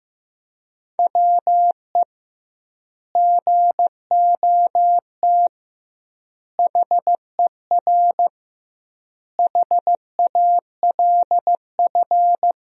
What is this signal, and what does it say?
Morse code, heard as a keyed tone, about 15 wpm, 700 Hz. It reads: WE GOT HER HALF
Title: 2x Word-Spacing - Sentences from Top 400 Words - 15wpm